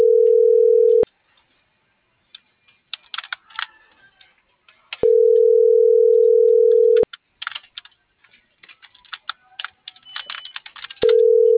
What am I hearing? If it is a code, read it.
Ringback tone (North America), 440 + 480 Hz.